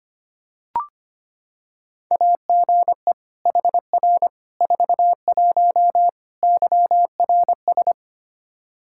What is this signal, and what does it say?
Morse code, heard as a keyed tone, about 25 wpm, 700 Hz. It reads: AGE HR 41 YRS
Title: Practice QSO - Exchange 3 - More Info - 25wpm